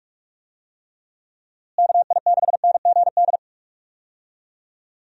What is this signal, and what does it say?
Morse code, heard as a keyed tone, about 46 wpm, 700 Hz. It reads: KI6NGD
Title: USA Call Signs - 6 Characters - Repeated 3x - 45wpm